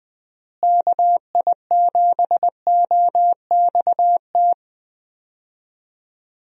Morse code, heard as a keyed tone, about 20 wpm, 700 Hz. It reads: KI7OXT
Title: USA Call Signs - 6 Characters - Repeated 2x - 20wpm